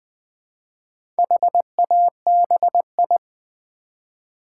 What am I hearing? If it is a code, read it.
Morse code, heard as a keyed tone, about 20 wpm, 700 Hz. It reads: HABI
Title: Build A Word - 1 Letter At A Time - 5-Letter Words - 20wpm